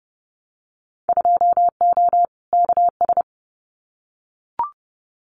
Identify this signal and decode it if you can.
Morse code, heard as a keyed tone, about 30 wpm, 700 Hz. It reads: 2O KS
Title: Winter Field Day Exchanges Encore 30wpm